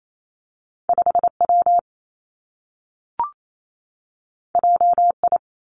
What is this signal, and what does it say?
Morse code, heard as a keyed tone, about 28 wpm, 700 Hz. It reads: 5 W J I
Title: Two Challenging Letter-Number Combinations 28wpm